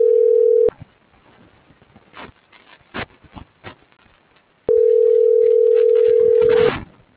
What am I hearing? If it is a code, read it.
Ringback tone (North America), 440 + 480 Hz.